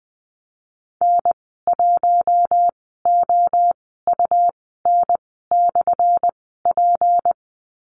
Morse code, heard as a keyed tone, about 20 wpm, spaced as for 14 wpm, 700 Hz. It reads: N1OUN/P
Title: 179 Lesson 29g - Slash - Call Signs 20wpm